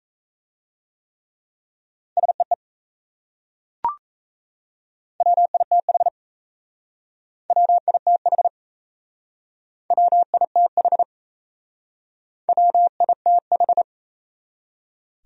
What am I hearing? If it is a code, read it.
Morse code, heard as a keyed tone, about 37 wpm, 700 Hz. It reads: SEE WITH WITH WITH WITH